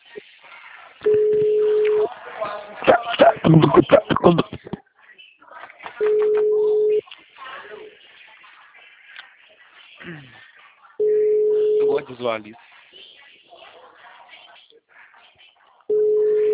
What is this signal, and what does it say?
Ringback tone (Europe, 425 Hz), with speech.
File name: out- -2021- - - .167